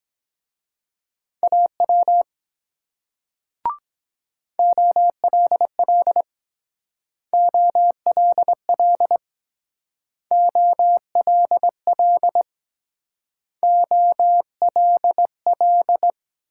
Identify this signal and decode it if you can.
Morse code, heard as a keyed tone, about 21 wpm, 700 Hz. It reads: AW OLL OLL OLL OLL
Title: Speed-Racing Multi-letter Phonemes - 17wpm